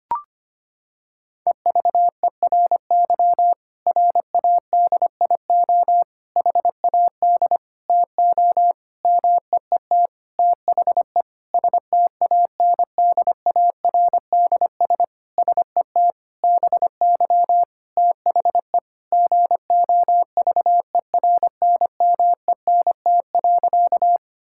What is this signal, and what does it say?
Morse code, heard as a keyed tone, about 25 wpm, 700 Hz. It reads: EVERY RADIO HAD TO MEET THE STANDARDS SET BY THE GOVERNMENT.
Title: Sentences from Top 1000 Words - Review Part 6 - 25wpm